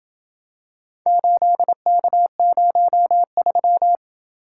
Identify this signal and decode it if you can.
Morse code, heard as a keyed tone, about 28 wpm, 700 Hz. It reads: 8K03